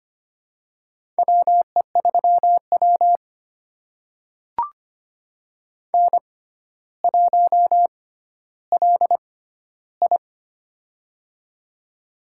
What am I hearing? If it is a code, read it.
Morse code, heard as a keyed tone, about 25 wpm, spaced as for 13 wpm, 700 Hz. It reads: WE3W N1LI
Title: USA Call Signs - 4 Characters - 6x Character Spacing - 25wpm